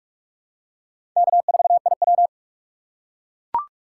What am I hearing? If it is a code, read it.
Morse code, heard as a keyed tone, about 42 wpm, 700 Hz. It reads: K4IW